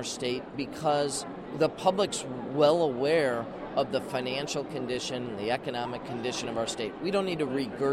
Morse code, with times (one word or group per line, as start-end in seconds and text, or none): none